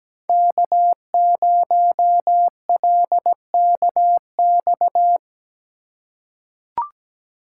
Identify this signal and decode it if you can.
Morse code, heard as a keyed tone, about 17 wpm, 700 Hz. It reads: K0LKX
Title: USA Call Signs - 5 Characters - 6x Character Spacing - 17wpm